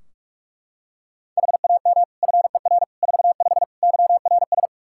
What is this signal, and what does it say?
Morse code, heard as a keyed tone, about 41 wpm, 700 Hz. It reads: HAM FER 45 YRS
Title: Practice QSO - Exchange 3 - More Info - 45wpm